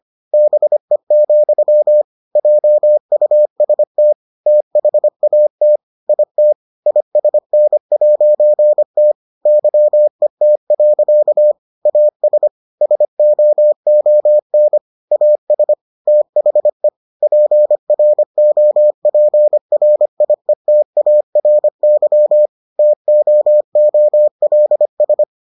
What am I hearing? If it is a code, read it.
Morse code, heard as a keyed tone, about 25 wpm, 600 Hz. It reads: BE, JUST THAT IT ISN'T YET. AS SOON AS THE PROPRIETARY TOOLS